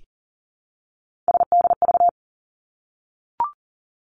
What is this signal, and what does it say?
Morse code, heard as a keyed tone, about 40 wpm, 700 Hz. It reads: SDV
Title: Three Challenging Letter-Number Combinations 40wpm